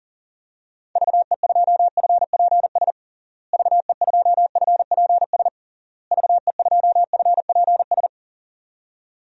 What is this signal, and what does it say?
Morse code, heard as a keyed tone, about 40 wpm, 700 Hz. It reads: VE2FPS VE2FPS VE2FPS
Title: DXCC-100 Call Signs - 6 Characters - Repeated 3x - 40wpm